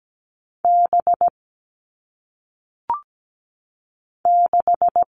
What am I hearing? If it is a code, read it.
Morse code, heard as a keyed tone, about 17 wpm, 700 Hz. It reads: B 6
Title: All Cut Numbers - 1-Digit 17wpm